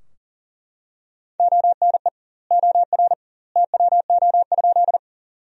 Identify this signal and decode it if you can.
Morse code, heard as a keyed tone, about 40 wpm, 700 Hz. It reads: ONE OR TWO?
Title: 2x Word-Spacing - Sentences from Top 100 Words - 40wpm